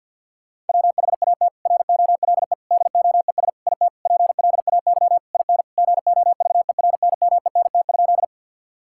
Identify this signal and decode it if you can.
Morse code, heard as a keyed tone, about 50 wpm, 700 Hz. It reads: WHAT ROLE DOES IT PLAY IN GOVERNMENT?
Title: Sentences from Top 1000 Words - Review Part 6 - 50wpm